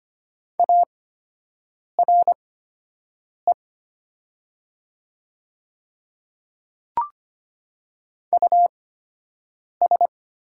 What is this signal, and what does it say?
Morse code, heard as a keyed tone, about 25 wpm, spaced as for 7 wpm, 700 Hz. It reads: ARE US
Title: 8x Character Spacing - Top 300 Words - 25wpm